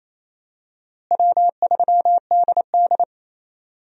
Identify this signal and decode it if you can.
Morse code, heard as a keyed tone, about 28 wpm, 700 Hz. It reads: W3DD
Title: USA Call Signs - 4 Characters - Repeated 2x - 28wpm